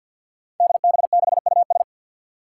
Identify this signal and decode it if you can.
Morse code, heard as a keyed tone, about 48 wpm, 700 Hz. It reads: DB6US
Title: Five Challenging Letter-Number Combinations 50wpm